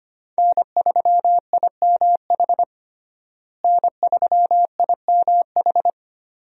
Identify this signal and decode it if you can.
Morse code, heard as a keyed tone, about 25 wpm, 700 Hz. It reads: N3IMH N3IMH
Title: USA Call Signs - 5 Characters - Repeated 2x - 25wpm